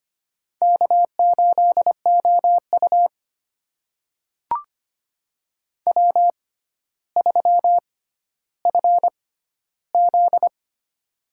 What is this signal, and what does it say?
Morse code, heard as a keyed tone, about 25 wpm, spaced as for 13 wpm, 700 Hz. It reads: K8OU W3FZ
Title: USA Call Signs - 4 Characters - 6x Character Spacing - 25wpm